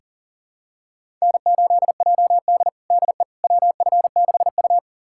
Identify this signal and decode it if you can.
Morse code, heard as a keyed tone, about 40 wpm, 700 Hz. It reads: N8JD DE WF6U
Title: USA Callsign DE Callsign - 4 Characters - 40wpm